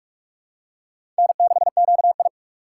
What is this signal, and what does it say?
Morse code, heard as a keyed tone, about 46 wpm, 700 Hz. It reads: N6QI
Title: USA Call Signs - 6x Character Spacing - 45wpm